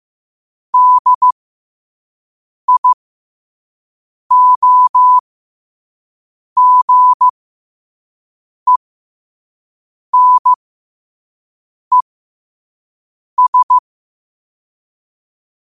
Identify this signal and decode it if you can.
Morse code, heard as a keyed tone, about 15 wpm, spaced as for 5 wpm, 1000 Hz. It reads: DIOGENES
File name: 5wpm